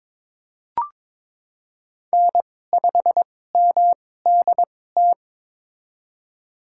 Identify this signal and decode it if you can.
Morse code, heard as a keyed tone, about 22 wpm, spaced as for 16 wpm, 700 Hz. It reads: N5MDT
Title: USA Call Signs - 2x Character Spacing - 22wpm